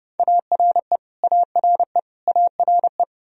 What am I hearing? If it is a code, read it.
Morse code, heard as a keyed tone, about 30 wpm, 700 Hz. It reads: ARE ARE ARE
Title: Top 100 Words Repeated 30wpm